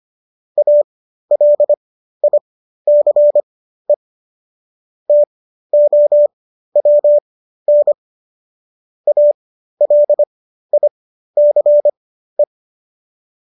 Morse code, heard as a keyed tone, about 25 wpm, spaced as for 13 wpm, 590 Hz. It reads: ALICE TOWN ALICE